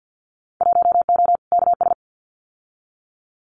Morse code, heard as a keyed tone, about 50 wpm, 700 Hz. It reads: 1O KS
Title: Winter Field Day Exchanges 50wpm